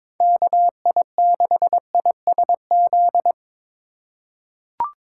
Morse code, heard as a keyed tone, about 22 wpm, 700 Hz. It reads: KI6ISZ
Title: USA Call Signs - 6 Characters - 2x Character Spacing - 22wpm